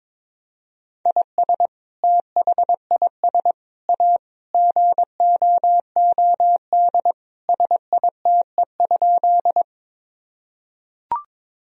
Morse code, heard as a keyed tone, about 22 wpm, 700 Hz. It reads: IS THIS A GOOD SITE?
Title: Sentences from Top 1000 Words - Review Part 4 - 22wpm